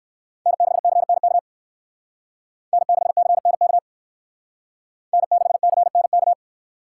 Morse code, heard as a keyed tone, about 58 wpm, 700 Hz. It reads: N6CNK N6CNK N6CNK